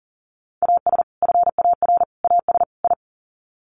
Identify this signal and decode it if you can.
Morse code, heard as a keyed tone, about 39 wpm, 700 Hz. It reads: AS FAR AS I
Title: Sets of 4 Words 40wpm